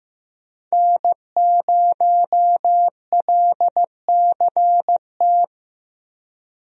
Morse code, heard as a keyed tone, about 15 wpm, 700 Hz. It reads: N0LCT
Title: USA Call Signs - 5 Characters - Repeated 2x - 15wpm